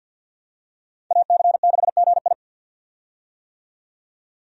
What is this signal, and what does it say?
Morse code, heard as a keyed tone, about 52 wpm, 700 Hz. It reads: AC6GI
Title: USA Call Signs - 5 Characters - 50wpm